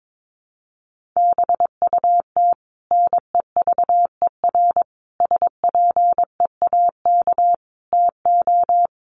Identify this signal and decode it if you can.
Morse code, heard as a keyed tone, about 22 wpm, 700 Hz. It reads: BUT NEVER SPEAK TO
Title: Sentences from Top 1100 Words 22wpm